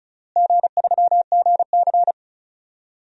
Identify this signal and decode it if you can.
Morse code, heard as a keyed tone, about 34 wpm, 700 Hz. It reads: G3GC
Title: DXCC-100 Call Signs - 4 Characters - Repeated 3x - 35wpm